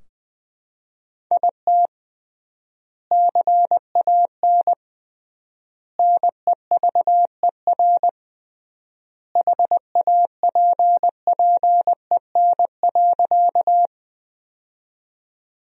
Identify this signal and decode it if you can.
Morse code, heard as a keyed tone, about 20 wpm, 700 Hz. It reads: IT CAN NEVER HAPPEN.
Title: 3x Word-Spacing - Sentences from Top 400 Words - 20wpm